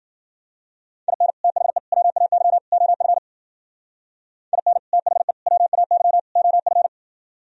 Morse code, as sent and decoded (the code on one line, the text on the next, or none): .. -. / - .... / .-- .- -.-- / --- ..-. / .. -. / - .... / .-- .- -.-- / --- ..-.
IN THE WAY OF IN THE WAY OF